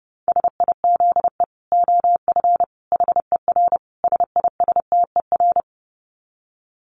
Morse code, heard as a keyed tone, about 28 wpm, 700 Hz. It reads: SIZE OF HER SISTER
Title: Dit Challenge 30wpm